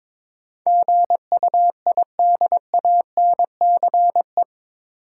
Morse code, heard as a keyed tone, about 22 wpm, 700 Hz. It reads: GUIDANCE
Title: Sentences from Top 1600 Words - New Word Drill - 22wpm